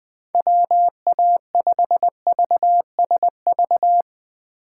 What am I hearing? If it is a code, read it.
Morse code, heard as a keyed tone, about 20 wpm, 700 Hz. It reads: WA5VSV